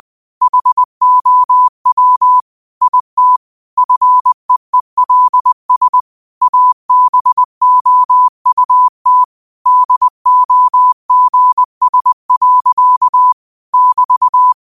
Morse code, heard as a keyed tone, about 20 wpm, 1000 Hz. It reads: HOW IT FEELS ABOUT DOGS. <BT>
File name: qotdfiles / 20wpm